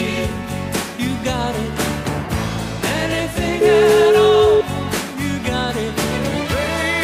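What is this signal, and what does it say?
Ringback tone (Europe, 425 Hz), with noise.
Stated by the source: O2 Czech Republic welcome melody (Uvítací melodie) that callers hear before the call is answered.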